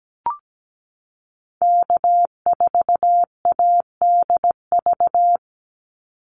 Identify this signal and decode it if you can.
Morse code, heard as a keyed tone, about 17 wpm, 700 Hz. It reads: K4ADV
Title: NA SOTA Activator Callsigns 17wpm